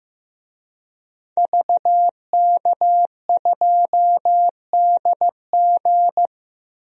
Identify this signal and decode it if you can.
Morse code, heard as a keyed tone, about 15 wpm, 700 Hz. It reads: VK2DG
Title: DXCC-100 Call Signs - 5 Characters - 15wpm